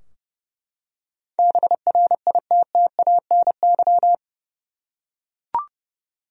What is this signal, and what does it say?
Morse code, heard as a keyed tone, about 30 wpm, 700 Hz. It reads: BRITTANY
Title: USA Common Names - 30wpm